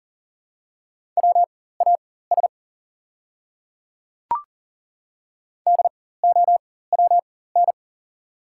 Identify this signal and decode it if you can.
Morse code, heard as a keyed tone, about 40 wpm, spaced as for 19 wpm, 700 Hz. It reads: WAS DOWN
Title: 4x Character Spacing - Top 100 Words - 40wpm